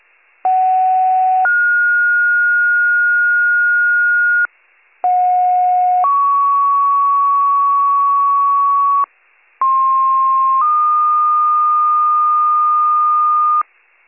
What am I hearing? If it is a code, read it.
Two-tone page, heard as a one-second tone, then a three-second tone, about 750 then 1430 Hz, about 730 then 1120 Hz, about 1060 then 1220 Hz. It is heard from 0.4 to 13.6 s.